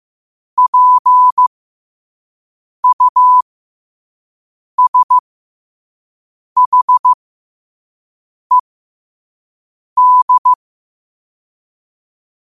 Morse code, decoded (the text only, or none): PUSHED